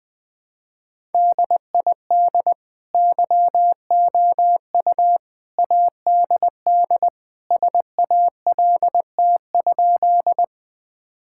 Morse code, heard as a keyed tone, about 21 wpm, 700 Hz. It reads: DID YOU ADD SALT?
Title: Sentences from Top 1000 Words - Review Part 5 - 20wpm